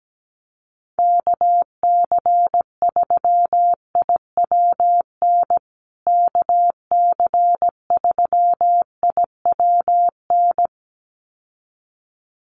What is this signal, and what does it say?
Morse code, heard as a keyed tone, about 17 wpm, 700 Hz. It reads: KC3IWN KC3IWN
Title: USA Call Signs Repeated 2x 17wpm